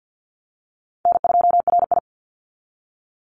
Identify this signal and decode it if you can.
Morse code, heard as a keyed tone, about 50 wpm, 700 Hz. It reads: N2RI
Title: Call Signs - 50wpm